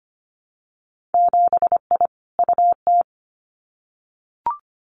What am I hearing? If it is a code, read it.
Morse code, heard as a keyed tone, about 25 wpm, 700 Hz. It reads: 7I UT
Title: Winter Field Day Exchanges Encore 25wpm